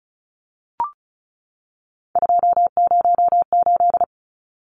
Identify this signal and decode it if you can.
Morse code, heard as a keyed tone, about 34 wpm, 700 Hz. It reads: 208
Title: Numbers - 3-Digit 35wpm